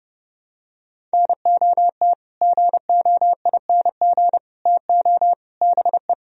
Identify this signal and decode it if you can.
Morse code, heard as a keyed tone, about 30 wpm, 700 Hz. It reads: NOT GOING TO BE